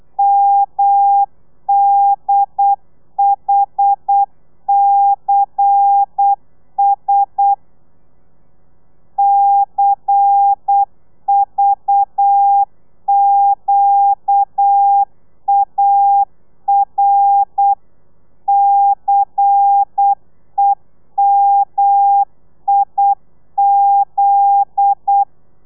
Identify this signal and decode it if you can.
Morse code, heard as a keyed tone, about 8 wpm, 800 Hz. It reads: MDHCS CVQAR CEMIZ
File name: telegrafia / i2tfj / Lez11 35